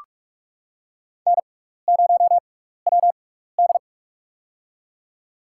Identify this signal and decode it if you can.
Morse code, heard as a keyed tone, about 46 wpm, spaced as for 16 wpm, 700 Hz. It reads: N0WD
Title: USA Call Signs - 4 Characters - 6x Character Spacing - 45wpm